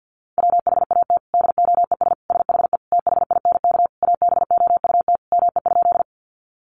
Morse code, heard as a keyed tone, about 50 wpm, 700 Hz. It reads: WHAT DOES SHE THINK ABOUT ME?
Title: Sentences from Top 200 Words 50wpm